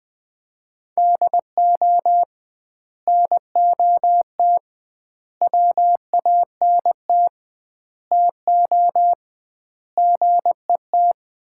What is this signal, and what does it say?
Morse code, heard as a keyed tone, about 20 wpm, 700 Hz. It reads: DO NOT WANT TO GET
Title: 2x Word-Spacing - Sets of 5 Words - 20wpm